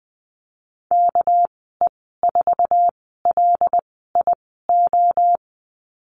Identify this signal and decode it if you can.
Morse code, heard as a keyed tone, about 20 wpm, 700 Hz. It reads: K E 4 L I O
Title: 238 Lesson 38g - Letter q - Call Signs 20wpm